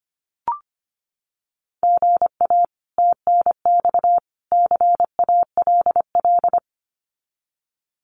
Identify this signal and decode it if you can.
Morse code, heard as a keyed tone, about 25 wpm, 700 Hz. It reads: GA TNX CALL
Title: Common QSO Phrases 25wpm